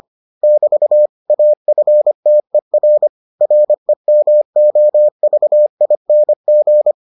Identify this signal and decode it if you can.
Morse code, heard as a keyed tone, about 25 wpm, 600 Hz. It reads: <BT> AFTER REMOVING